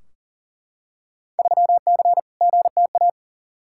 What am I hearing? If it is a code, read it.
Morse code, heard as a keyed tone, about 40 wpm, 700 Hz. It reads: 3C GTA
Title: Field Day Exchanges Encore - 40wpm